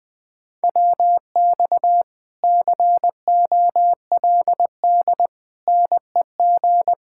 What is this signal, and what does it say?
Morse code, heard as a keyed tone, about 21 wpm, 700 Hz. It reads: WX COLD NEG
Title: Speed-Racing Common QSO Phrases - 20wpm